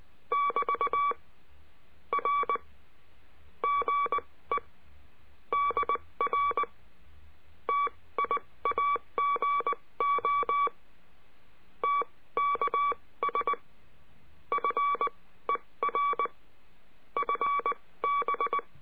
Morse code, heard as a keyed tone, about 19 wpm, spaced as for 16 wpm, 1140 Hz. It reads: <BT> R GE DR TIAGO TKS FER FB
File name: cursos / CURSO DE TELEGRAFIA / Telegrafia - Trilha 22